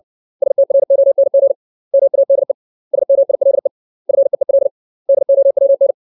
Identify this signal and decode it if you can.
Morse code, heard as a keyed tone, about 60 wpm, 550 Hz. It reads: STRONG MADE SMILE FEEL DOWN
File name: 632 words / 632 Words 60WPM